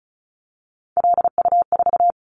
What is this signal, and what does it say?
Morse code, heard as a keyed tone, about 36 wpm, 700 Hz. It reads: LU4
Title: Callsign Prefixes for Top 50 DXCC - 35wpm